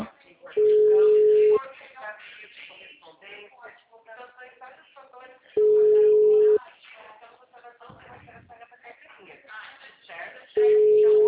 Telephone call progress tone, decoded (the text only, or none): ringback tone